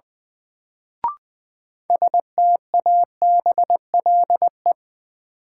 Morse code, heard as a keyed tone, about 20 wpm, 700 Hz. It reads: STABLE
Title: Sentences from Top 1500 Words - New Word Drill Alt - 20wpm